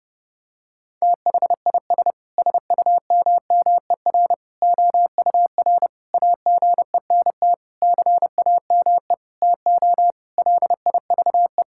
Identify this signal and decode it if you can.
Morse code, heard as a keyed tone, about 30 wpm, 700 Hz. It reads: THIS SUMMER OUR AGENT CAME TO LIVE
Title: Sentences from Top 1600 Words 30wpm